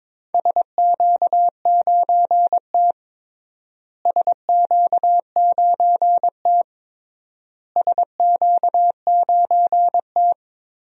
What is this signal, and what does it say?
Morse code, heard as a keyed tone, about 22 wpm, 700 Hz. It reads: SQ9T SQ9T SQ9T
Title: DXCC-100 Call Signs - 4 Characters - Repeated 3x - 22wpm